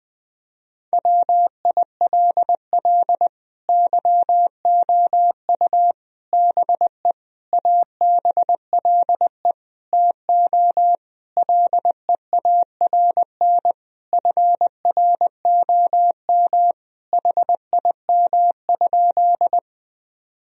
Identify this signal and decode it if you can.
Morse code, heard as a keyed tone, about 20 wpm, 700 Hz. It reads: WILL YOU BE ABLE TO LEARN FROM HIM?
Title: Sentences from Top 1000 Words - Review Part 2 - 20wpm